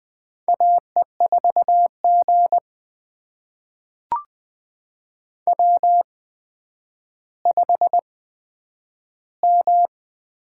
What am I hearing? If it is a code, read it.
Morse code, heard as a keyed tone, about 20 wpm, spaced as for 10 wpm, 700 Hz. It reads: AE4G W5M
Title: USA Call Signs - 4 Characters - 8x Character Spacing - 20wpm